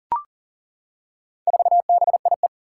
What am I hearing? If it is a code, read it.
Morse code, heard as a keyed tone, about 38 wpm, 700 Hz. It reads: 4BIE
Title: Four Challenging Letter-Number Combinations 40wpm